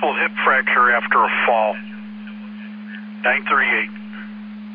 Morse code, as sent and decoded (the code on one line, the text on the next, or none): none